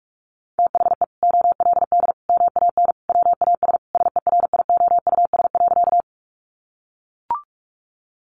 Morse code, heard as a keyed tone, about 44 wpm, 700 Hz. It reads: THE OLD MAN WAS SERIOUS.
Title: Sentences from Top 1100 Words 45wpm